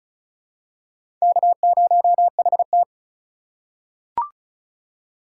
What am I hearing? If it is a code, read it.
Morse code, heard as a keyed tone, about 34 wpm, 700 Hz. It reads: K0HT
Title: K1USN SST QSO Practice - 35wpm